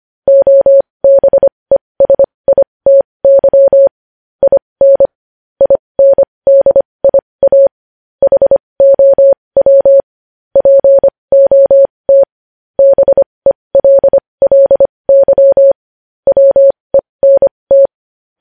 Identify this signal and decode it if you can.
Morse code, heard as a keyed tone, about 25 wpm, spaced as for 20 wpm, 550 Hz. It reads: OBESITY IN INDIA HOW POT BELLY WENT
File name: podcast / News headlines-1x-25x20